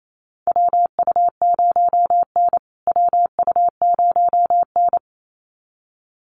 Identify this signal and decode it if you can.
Morse code, heard as a keyed tone, about 28 wpm, 700 Hz. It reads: WU0N WU0N
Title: USA Call Signs Repeated 2x 28wpm